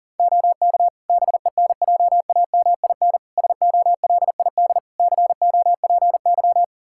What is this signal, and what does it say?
Morse code, heard as a keyed tone, about 40 wpm, 700 Hz. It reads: OK BENJAMIN SOLID COPY
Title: Practice QSO - Exchange 3 - More Info - 40wpm